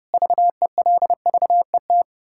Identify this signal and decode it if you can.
Morse code, heard as a keyed tone, about 32 wpm, 700 Hz. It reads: VELVET
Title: Pangram Challenge v2 - 30wpm